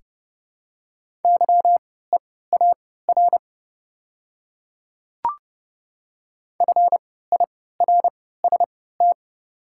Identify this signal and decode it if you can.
Morse code, heard as a keyed tone, about 30 wpm, spaced as for 17 wpm, 700 Hz. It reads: YEAR FIRST
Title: 3x Character Spacing - Top 300 Words - 30wpm